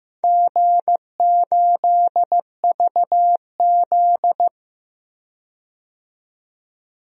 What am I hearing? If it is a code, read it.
Morse code, heard as a keyed tone, about 16 wpm, 700 Hz. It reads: G8VZ
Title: DXCC-100 Call Signs - 4 Characters - 15wpm